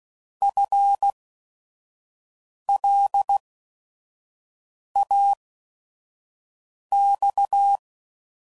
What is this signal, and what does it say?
Morse code, heard as a keyed tone, about 16 wpm, spaced as for 5 wpm, 800 Hz. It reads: FLAX
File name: Prefixes / By First Letter / Fs